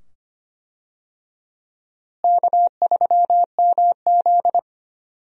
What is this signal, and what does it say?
Morse code, heard as a keyed tone, about 25 wpm, 700 Hz. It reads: K3MZ
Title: USA Call Signs - 4 Characters - Repeated 2x - 25wpm